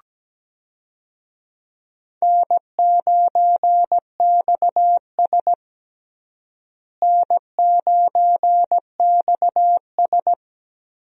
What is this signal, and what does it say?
Morse code, heard as a keyed tone, about 17 wpm, 700 Hz. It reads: N9XS N9XS
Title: USA Call Signs - 4 Characters - Repeated 3x - 17wpm